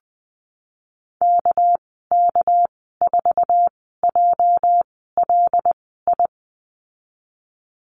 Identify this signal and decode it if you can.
Morse code, heard as a keyed tone, about 20 wpm, spaced as for 15 wpm, 700 Hz. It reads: KK4JLI
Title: 213 Lesson 34g - Number 8 - Call Signs 20wpm